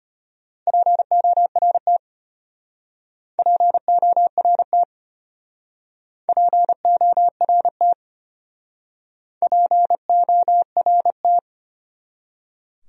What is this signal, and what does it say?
Morse code, heard as a keyed tone, about 31 wpm, 700 Hz. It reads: PORT PORT PORT PORT
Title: Speed-Racing Top 200 Words 25wpm